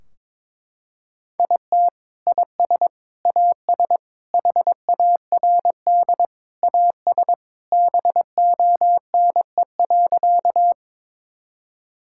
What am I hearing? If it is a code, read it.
Morse code, heard as a keyed tone, about 21 wpm, 700 Hz. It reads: IT IS AS HARD AS BONE.
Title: Sentences from Top 1600 Words 22wpm